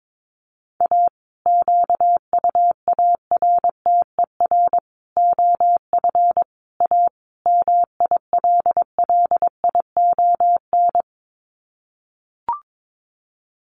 Morse code, heard as a keyed tone, about 22 wpm, 700 Hz. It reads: A QUARTER OF A MILLION
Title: Sets of 5 Words Encore2 22wpm